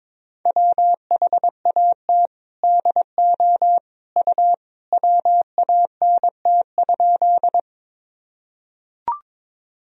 Morse code, heard as a keyed tone, about 22 wpm, 700 Hz. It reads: WHAT DO U WANT?
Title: Sentences from Top 200 Words and Abbreviations 22wpm